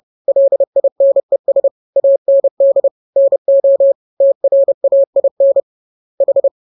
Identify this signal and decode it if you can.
Morse code, heard as a keyed tone, about 30 wpm, 550 Hz. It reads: LINES AND NO TRAIN H